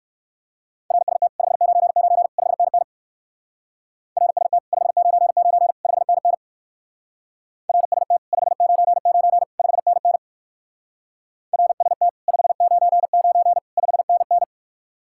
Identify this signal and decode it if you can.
Morse code, heard as a keyed tone, about 59 wpm, 700 Hz. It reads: RST 599 5NN RST 599 5NN RST 599 5NN RST 599 5NN